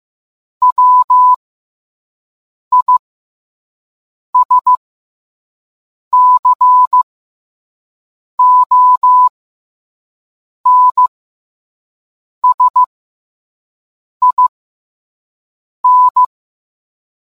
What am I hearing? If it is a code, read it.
Morse code, heard as a keyed tone, about 15 wpm, spaced as for 5 wpm, 1000 Hz. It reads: WISCONSIN